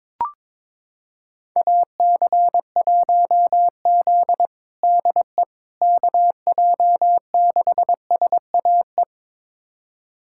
Morse code, heard as a keyed tone, about 22 wpm, 700 Hz. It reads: AC1Z DE KJ6SAE
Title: USA Callsign DE Callsign 22wpm